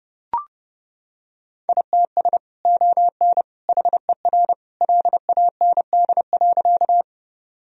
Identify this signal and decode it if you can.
Morse code, heard as a keyed tone, about 28 wpm, 700 Hz. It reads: ITS ON HER LAND.
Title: Sentences from Top 1000 Words - Review Part 1 - 30wpm